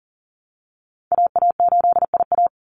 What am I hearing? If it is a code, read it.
Morse code, heard as a keyed tone, about 40 wpm, 700 Hz. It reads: AA8IA
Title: Call Signs Repeated 3x - 40wpm